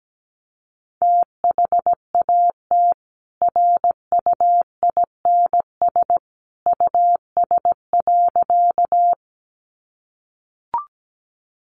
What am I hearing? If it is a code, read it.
Morse code, heard as a keyed tone, about 17 wpm, 700 Hz. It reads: THAT RUINS US.
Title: A Sound Of Thunder 17wpm